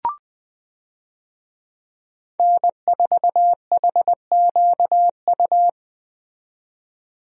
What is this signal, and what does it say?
Morse code, heard as a keyed tone, about 21 wpm, 700 Hz. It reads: N4HQU
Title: USA Call Signs - 5 Characters - 20wpm